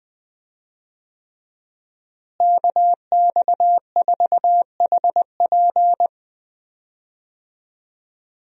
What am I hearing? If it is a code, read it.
Morse code, heard as a keyed tone, about 19 wpm, 700 Hz. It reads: KX4HP
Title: USA Call Signs - 5 Characters - Repeated 3x - 20wpm